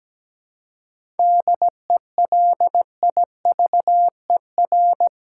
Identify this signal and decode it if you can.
Morse code, heard as a keyed tone, about 17 wpm, 700 Hz. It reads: DELIVER
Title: Seven Letter Words - 17wpm